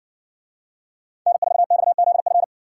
Morse code, heard as a keyed tone, about 61 wpm, 700 Hz. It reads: N4XZF